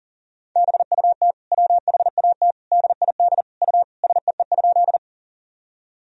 Morse code, heard as a keyed tone, about 40 wpm, 700 Hz. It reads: BUT WHAT DID U SEE?
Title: Sentences from Top 100 Words and Abbreviations 40wpm